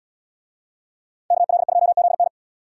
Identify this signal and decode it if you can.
Morse code, heard as a keyed tone, about 77 wpm, 690 Hz. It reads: KL2ZN